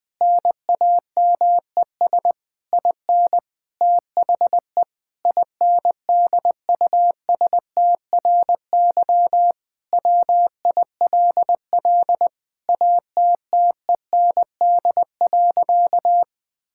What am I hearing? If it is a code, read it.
Morse code, heard as a keyed tone, about 20 wpm, 700 Hz. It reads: NAMES IN THE INDUSTRY WILL ATTEND.
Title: Sentences with 7-Word Phrases - Spoken Beginning - 20wpm